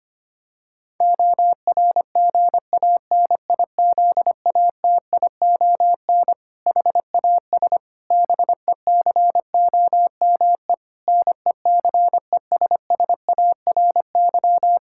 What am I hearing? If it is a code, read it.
Morse code, heard as a keyed tone, about 25 wpm, 700 Hz. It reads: ORGANIZATION HAS BECOME NECESSARY
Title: Sentences from Top 1000 Words - Review Part 5 - 25wpm